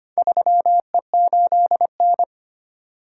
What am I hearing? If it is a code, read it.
Morse code, heard as a keyed tone, about 25 wpm, 700 Hz. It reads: 3E8N
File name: practice / 25wpm / Four Letter-Number Combinations - 25wpm